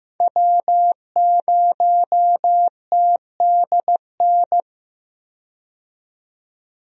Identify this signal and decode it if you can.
Morse code, heard as a keyed tone, about 15 wpm, 700 Hz. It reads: W0TDN
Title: USA Call Signs - 5 Characters - 15wpm